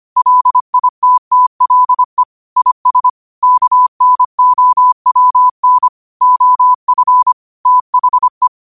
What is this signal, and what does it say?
Morse code, heard as a keyed tone, about 25 wpm, 1000 Hz. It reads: LITTLE IS KNOWN OF THE
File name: qotdfiles / 25wpm